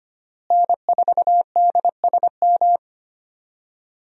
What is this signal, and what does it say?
Morse code, heard as a keyed tone, about 25 wpm, 700 Hz. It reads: N4DSM
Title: USA Call Signs - 5 Characters - Repeated 3x - 25wpm